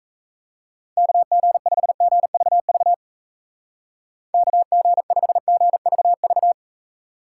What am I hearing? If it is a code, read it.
Morse code, heard as a keyed tone, about 34 wpm, 700 Hz. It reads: KG5GVV KG5GVV
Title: Speed Racing - USA 2x3 Call Signs 28wpm